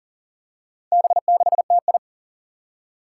Morse code, heard as a keyed tone, about 40 wpm, 700 Hz. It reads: B6TI
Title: Four Challenging Letter-Number Combinations 40wpm